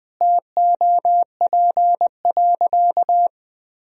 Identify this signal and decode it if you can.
Morse code, heard as a keyed tone, about 21 wpm, 700 Hz. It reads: TOP.
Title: Sentences from Top 1000 Words - Review Part 3 - 20wpm